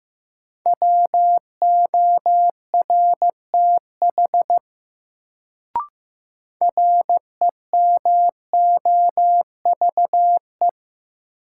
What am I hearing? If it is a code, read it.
Morse code, heard as a keyed tone, about 15 wpm, 700 Hz. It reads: WORTH REMOVE
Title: Sentences from Top 800 Words - New Word Drill - 15wpm